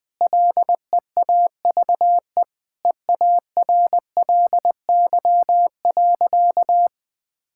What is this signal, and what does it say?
Morse code, heard as a keyed tone, about 21 wpm, 700 Hz. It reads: LEAVE EARLY.
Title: Sentences with 5-Word Phrases - Spoken Beginning - 20wpm